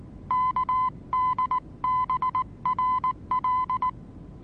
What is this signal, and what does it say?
Morse code, heard as a keyed tone, about 19 wpm, 1010 Hz. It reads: KDBRL
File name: repos / Morsekurs / DJ4UF Morsen26